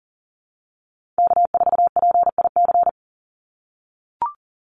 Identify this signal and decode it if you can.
Morse code, heard as a keyed tone, about 40 wpm, 700 Hz. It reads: K4PIC
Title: NA SOTA Chaser Callsigns 40wpm